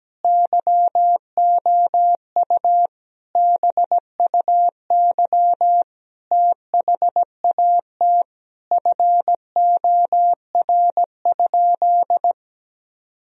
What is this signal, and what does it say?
Morse code, heard as a keyed tone, about 17 wpm, 700 Hz. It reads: YOU BUY THAT FOR?
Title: Sentences from Top 1000 Words - Review Part 4 - 17wpm